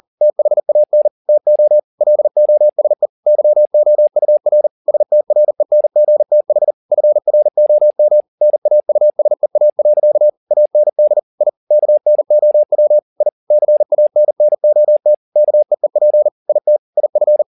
Morse code, heard as a keyed tone, about 40 wpm, 600 Hz. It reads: THAN TO LOSE YOUR STRENGTH FROM NAUSEA. AND I KNOW I CANNOT KEEP IT IF
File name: the old man and the sea 2 40wpm